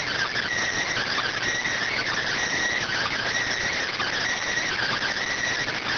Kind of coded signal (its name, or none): Emergency Alert System (SAME) tones